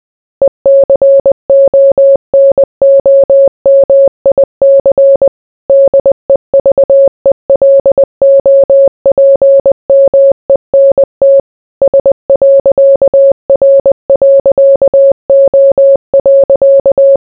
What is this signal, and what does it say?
Morse code, heard as a keyed tone, about 20 wpm, 550 Hz. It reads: ECONOMIC DEVELOPMENT S.R.O.